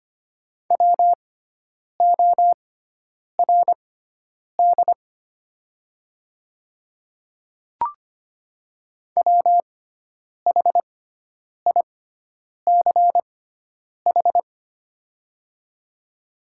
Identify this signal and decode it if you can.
Morse code, heard as a keyed tone, about 25 wpm, spaced as for 9 wpm, 690 Hz. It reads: WORD WHICH
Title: 6x Character Spacing - Top 200 Words - 25wpm